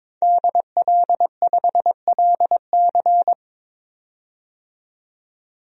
Morse code, heard as a keyed tone, about 22 wpm, 700 Hz. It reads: DL5LC
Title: DXCC-100 Call Signs - 5 Characters - 22wpm